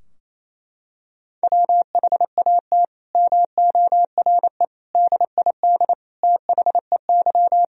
Morse code, heard as a keyed tone, about 28 wpm, 700 Hz. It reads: WHAT MORE DID THEY
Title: Sentences from Top 1000 Words - Review Part 3 - 28wpm